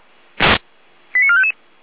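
Five-tone selective call (ZVEI), heard as quick runs of five tones, tones of about 70 ms. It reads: 89455
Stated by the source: Five-tone ZVEI selective call, received with an SDR, recorded by feuerwehr-orlamuende (fire brigade Orlamünde).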